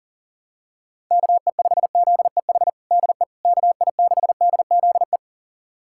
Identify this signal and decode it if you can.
Morse code, heard as a keyed tone, about 40 wpm, 700 Hz. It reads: KE5ZEH DE KI6DZE
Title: USA Callsign DE Callsign 40wpm